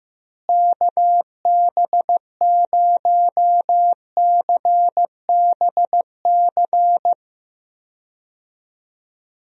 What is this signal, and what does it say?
Morse code, heard as a keyed tone, about 15 wpm, 700 Hz. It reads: KB0CBC
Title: USA Call Signs - 6 Characters - Repeated 2x - 15wpm